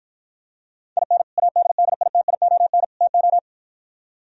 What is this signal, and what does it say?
Morse code, heard as a keyed tone, about 52 wpm, 700 Hz. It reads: IN ADDITION TO